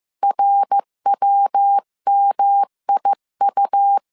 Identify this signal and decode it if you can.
Morse code, heard as a keyed tone, about 15 wpm, 800 Hz. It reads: RWMIU
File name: CW / DET / 1 3 1500C S C 14wpm 2458